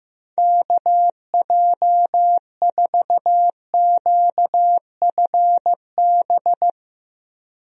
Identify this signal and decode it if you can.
Morse code, heard as a keyed tone, about 16 wpm, 700 Hz. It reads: KJ4QFB
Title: USA Call Signs - 6 Characters - 15wpm